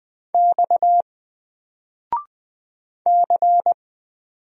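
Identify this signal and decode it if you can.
Morse code, heard as a keyed tone, about 20 wpm, 700 Hz. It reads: X C
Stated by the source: Single letter and number Morse drill at 20 wpm, from N4LCD.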